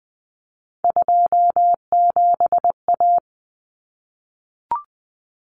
Morse code, heard as a keyed tone, about 20 wpm, 700 Hz. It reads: 27A
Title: Common Cut Numbers - 3-Digit 20wpm